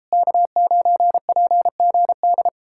Morse code, heard as a keyed tone, about 37 wpm, 700 Hz. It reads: K9PGD